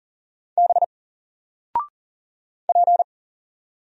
Morse code, heard as a keyed tone, about 40 wpm, 700 Hz. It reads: B P
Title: Single Letter-Number Alt 40wpm